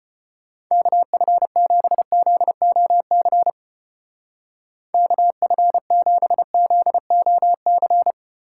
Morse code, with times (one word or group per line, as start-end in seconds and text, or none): 0.7-3.5 KF7ZOC
4.9-8.1 KF7ZOC